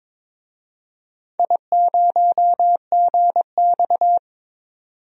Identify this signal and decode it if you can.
Morse code, heard as a keyed tone, about 21 wpm, 700 Hz. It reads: I0GX
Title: DXCC-100 Call Signs - 4 Characters - 22wpm